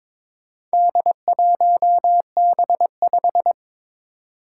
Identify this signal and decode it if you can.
Morse code, heard as a keyed tone, about 22 wpm, 700 Hz. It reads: D1B5
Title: Four Challenging Letter-Number Combinations 22wpm